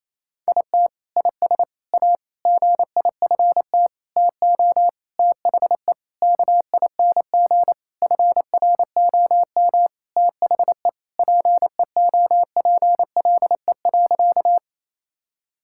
Morse code, heard as a keyed tone, about 28 wpm, 700 Hz. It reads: IT IS A GIFT TO THE KING FROM THE PEOPLE.
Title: Sentences from Top 1000 Words - Review Part 6 - 28wpm